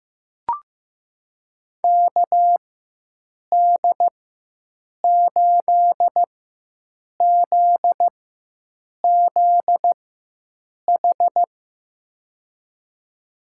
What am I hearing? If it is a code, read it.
Morse code, heard as a keyed tone, about 16 wpm, spaced as for 7 wpm, 700 Hz. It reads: KD8ZZH
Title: USA Call Signs - 6 Characters - 4x Character Spacing - 15wpm